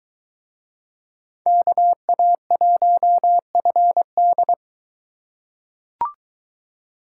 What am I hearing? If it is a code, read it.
Morse code, heard as a keyed tone, about 23 wpm, 700 Hz. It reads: KA1FD